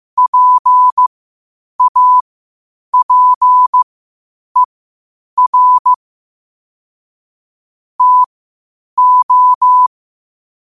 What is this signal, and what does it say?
Morse code, heard as a keyed tone, about 15 wpm, spaced as for 8 wpm, 1000 Hz. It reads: PAPER TO
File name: qotdfiles / 8wpm